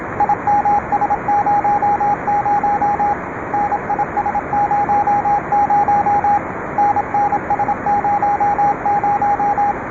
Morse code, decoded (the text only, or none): IMS00 NIS00 NNS00